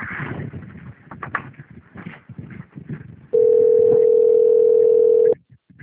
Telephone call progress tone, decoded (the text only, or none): ringback tone